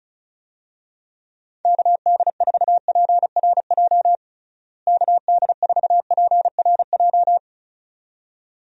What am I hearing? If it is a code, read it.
Morse code, heard as a keyed tone, about 34 wpm, 700 Hz. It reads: KD4PRJ KD4PRJ
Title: USA Call Signs - 6 Characters - Repeated 2x - 35wpm